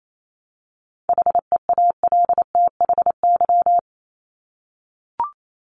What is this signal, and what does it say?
Morse code, heard as a keyed tone, about 28 wpm, 700 Hz. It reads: HEALTHY
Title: Top 501-750 Words 28wpm